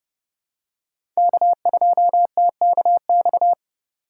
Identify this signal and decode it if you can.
Morse code, heard as a keyed tone, about 30 wpm, 700 Hz. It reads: K2TKX
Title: USA Call Signs - 5 Characters - 4x Character Spacing - 30wpm